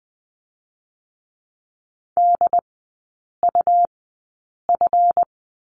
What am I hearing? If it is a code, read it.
Morse code, heard as a keyed tone, about 20 wpm, spaced as for 8 wpm, 700 Hz. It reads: DUF
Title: 178 Lesson 29e - Slash - Three Characters 20wpm